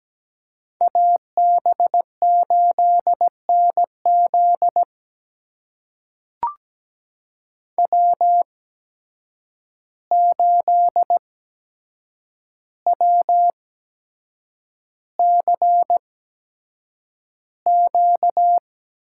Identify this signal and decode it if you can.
Morse code, heard as a keyed tone, about 17 wpm, spaced as for 7 wpm, 700 Hz. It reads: AB8NZ W8WCQ
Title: USA Call Signs - 5 Characters - 8x Character Spacing - 17wpm